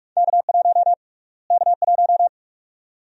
Morse code, heard as a keyed tone, about 46 wpm, 700 Hz. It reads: K1 K1
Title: USA Call Sign Prefixes - Repeated 2x - 45wpm